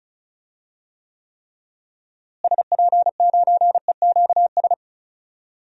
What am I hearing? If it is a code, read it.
Morse code, heard as a keyed tone, about 36 wpm, 700 Hz. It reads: SP9EQS
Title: DXCC-100 Call Signs - 6 Characters - Repeated 3x - 35wpm